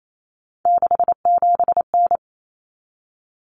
Morse code, heard as a keyed tone, about 28 wpm, 700 Hz. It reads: 67N
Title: Common Cut Numbers - 3-Digit 28wpm